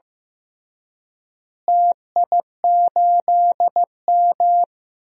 Morse code, heard as a keyed tone, about 15 wpm, 700 Hz. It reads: TI8M
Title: DXCC-100 Call Signs - 4 Characters - Repeated 2x - 15wpm